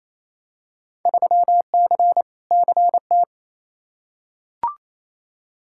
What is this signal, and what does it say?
Morse code, heard as a keyed tone, about 28 wpm, 700 Hz. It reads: 3C CT
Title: Field Day Exchanges Encore - 28wpm